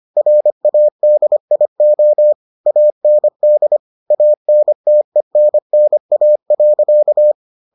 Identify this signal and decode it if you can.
Morse code, heard as a keyed tone, about 25 wpm, 600 Hz. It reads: RADIO AND ANTENNA.